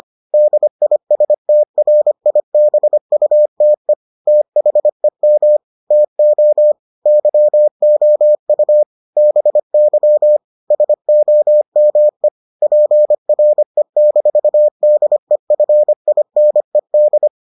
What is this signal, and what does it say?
Morse code, heard as a keyed tone, about 25 wpm, 600 Hz. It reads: DISTRIBUTE THEM TO YOU BY SOME PRE-DEFINED